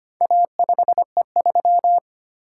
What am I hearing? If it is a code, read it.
Morse code, heard as a keyed tone, about 25 wpm, 700 Hz. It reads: A5E3
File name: practice / 25wpm / Four Letter-Number Combinations - 25wpm